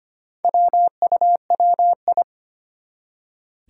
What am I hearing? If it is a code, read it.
Morse code, heard as a keyed tone, about 25 wpm, 700 Hz. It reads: WUWI
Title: Four Challenging Letter-Number Combinations 25wpm